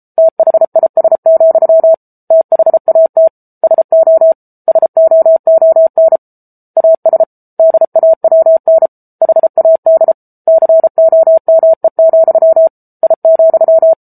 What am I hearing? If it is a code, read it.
Morse code, heard as a keyed tone, about 34 wpm, 650 Hz. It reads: THIS, THAT SO SOON AS DAWN HAD COME, I,
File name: Book Two / 07-Chapter 7 - 33 WPM